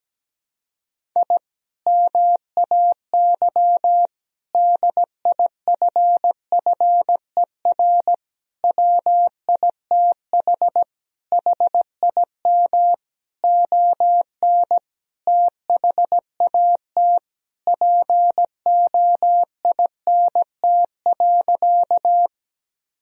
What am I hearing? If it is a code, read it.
Morse code, heard as a keyed tone, about 17 wpm, 700 Hz. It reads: I MAY DIFFER WITH HIM ON THAT POINT.
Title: Sentences from Top 1000 Words - Review Part 1 - 17wpm